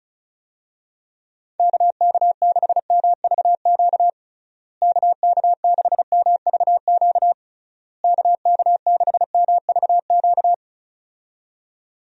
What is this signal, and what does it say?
Morse code, heard as a keyed tone, about 36 wpm, 700 Hz. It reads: KK6MVQ KK6MVQ KK6MVQ
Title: USA Call Signs - 6 Characters - Repeated 3x - 35wpm